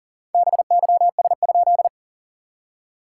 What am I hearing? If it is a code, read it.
Morse code, heard as a keyed tone, about 39 wpm, 690 Hz. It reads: BYS?
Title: SOTA QSO Practice 40wpm